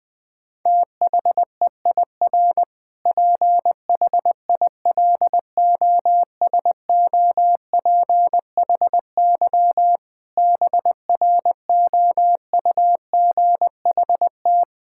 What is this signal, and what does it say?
Morse code, heard as a keyed tone, about 21 wpm, 700 Hz. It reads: THEIR PHILOSOPHY BROUGHT
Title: Sentences from Top 1000 Words - Review Part 6 - 20wpm